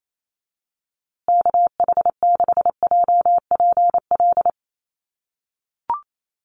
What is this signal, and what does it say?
Morse code, heard as a keyed tone, about 28 wpm, 700 Hz. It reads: KH6JPL
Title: USA Call Signs Repeated 3x 28wpm